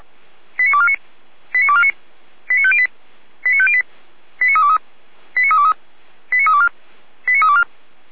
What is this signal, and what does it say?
Five-tone selective call (ZVEI), heard as quick runs of five tones, tones of about 70 ms. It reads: 89249 89249 89508 89508 89323 89323 89324 89324
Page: audio.f85360.feuerwehr-orlamuende.org